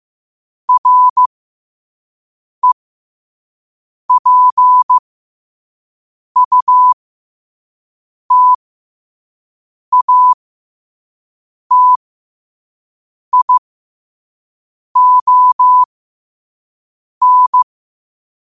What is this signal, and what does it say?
Morse code, heard as a keyed tone, about 15 wpm, spaced as for 5 wpm, 1000 Hz. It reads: REPUTATION
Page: New quotes every day in morse code at 5 Words per minute.